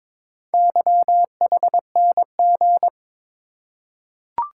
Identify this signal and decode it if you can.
Morse code, heard as a keyed tone, about 22 wpm, 700 Hz. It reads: YHNG
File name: practice / 22wpm / Four Letter-Number Combinations - 22wpm